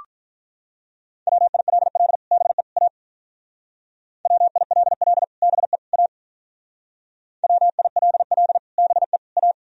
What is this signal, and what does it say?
Morse code, heard as a keyed tone, about 44 wpm, 700 Hz. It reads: WILL BE A WILL BE A WILL BE A